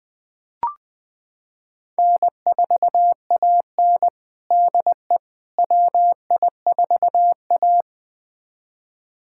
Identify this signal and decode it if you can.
Morse code, heard as a keyed tone, about 20 wpm, 700 Hz. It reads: N4AN DE WI4A
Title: USA Callsign DE Callsign - 4 Characters - 20wpm